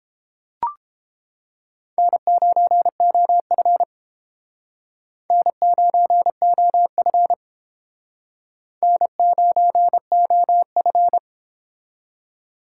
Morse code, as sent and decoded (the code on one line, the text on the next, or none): -. ----. --- ..-. / -. ----. --- ..-. / -. ----. --- ..-.
N9OF N9OF N9OF